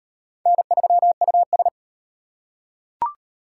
Morse code, heard as a keyed tone, about 35 wpm, 700 Hz. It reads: N3US